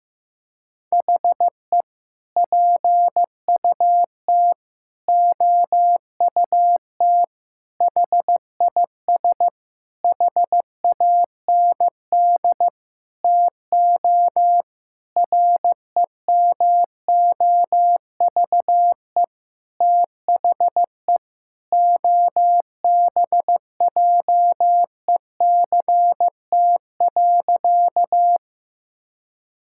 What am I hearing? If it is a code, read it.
Morse code, heard as a keyed tone, about 14 wpm, 700 Hz. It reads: HE PUT OUT HIS HAND TO REMOVE THE OBJECT.
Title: Sentences from Top 1000 Words - Review Part 5 - 15wpm